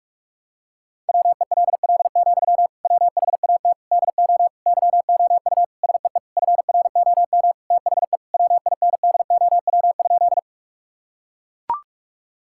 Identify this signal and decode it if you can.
Morse code, heard as a keyed tone, about 41 wpm, 700 Hz. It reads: WELL, WHAT DO YOU SEE FROM THE WINDOW?
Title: Sentences from Top 1600 Words 45wpm